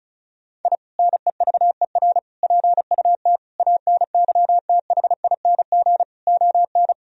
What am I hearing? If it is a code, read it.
Morse code, heard as a keyed tone, about 34 wpm, 700 Hz. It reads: I NEVER PUT ANYTHING ON
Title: Sentences from Top 1000 Words - Review Part 1 - 35wpm